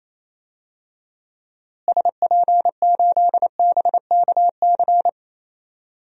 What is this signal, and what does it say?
Morse code, heard as a keyed tone, about 27 wpm, 700 Hz. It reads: SP8BKC
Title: DXCC-100 Call Signs - 6 Characters - Repeated 2x - 28wpm